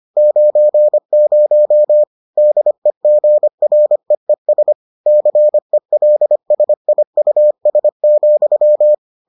Morse code, heard as a keyed tone, about 25 wpm, 600 Hz. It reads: 90 DEGREES CELSIUS,